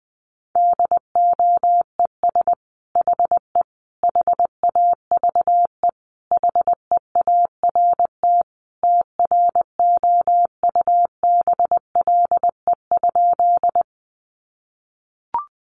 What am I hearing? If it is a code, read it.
Morse code, heard as a keyed tone, about 20 wpm, 700 Hz. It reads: DOES HE HAVE HEART TROUBLE?
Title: Sentences from Top 1200 Words 20wpm